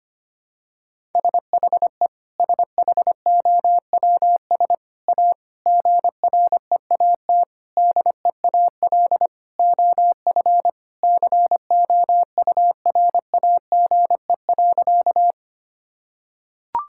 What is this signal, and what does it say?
Morse code, heard as a keyed tone, about 25 wpm, 700 Hz. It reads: SHE SHOWS A GREAT DEAL OF COURAGE.
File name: practice / 25wpm / Sentences with 4-Word Phrases - 25wpm